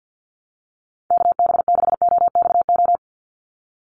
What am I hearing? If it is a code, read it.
Morse code, heard as a keyed tone, about 50 wpm, 700 Hz. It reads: KB6OXO
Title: USA Call Signs 50wpm